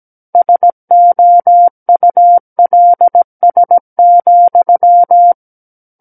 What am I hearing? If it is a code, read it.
Morse code, heard as a keyed tone, about 17 wpm, 700 Hz. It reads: SOULS,
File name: Book One / 08-Chapter 8 - 17 WPM